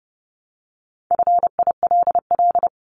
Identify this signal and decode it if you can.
Morse code, heard as a keyed tone, about 30 wpm, 700 Hz. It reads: FILL
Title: Top 500 Words Repeated 30wpm